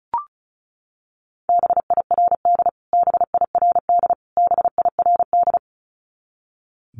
Morse code, heard as a keyed tone, about 34 wpm, 700 Hz. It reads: BIRD BIRD BIRD
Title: Top 500 Words Repeated 35wpm